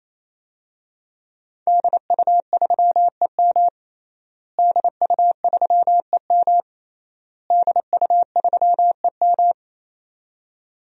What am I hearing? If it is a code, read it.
Morse code, heard as a keyed tone, about 28 wpm, 700 Hz. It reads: DU3EM DU3EM DU3EM
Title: DXCC-100 Call Signs - 5 Characters - Repeated 3x - 28wpm